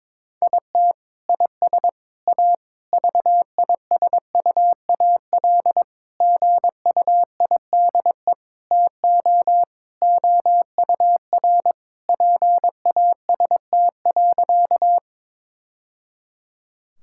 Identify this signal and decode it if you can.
Morse code, heard as a keyed tone, about 22 wpm, 700 Hz. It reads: IT IS A VISUAL GUIDE TO OUR PAST.
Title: Sentences from Top 1000 Words - Review Part 6 - 22wpm